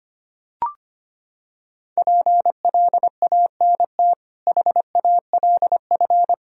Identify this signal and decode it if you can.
Morse code, heard as a keyed tone, about 25 wpm, 700 Hz. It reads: PLANT HALF
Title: Sentences from Top 1000 Words - Review Part 1 - 25wpm